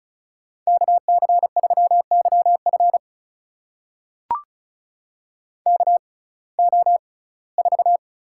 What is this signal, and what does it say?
Morse code, heard as a keyed tone, about 34 wpm, spaced as for 21 wpm, 700 Hz. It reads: KC3YF KO4
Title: USA Call Signs - 5 Characters - 6x Character Spacing - 35wpm